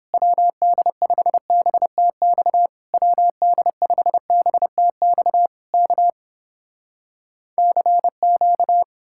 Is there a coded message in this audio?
Yes: Morse code from 0.1 to 8.8 s, heard as a keyed tone, about 30 wpm, 700 Hz. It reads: WD5BTX WD5BTX K CQ